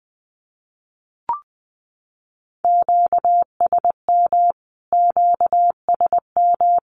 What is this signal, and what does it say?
Morse code, heard as a keyed tone, about 20 wpm, 700 Hz. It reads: QSM QSM
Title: Q-Codes Repeated 3x 20wpm